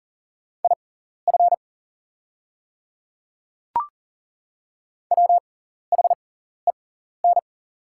Morse code, heard as a keyed tone, about 41 wpm, spaced as for 14 wpm, 700 Hz. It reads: IF WHEN
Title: 6x Character Spacing - Top 100 Words - 40wpm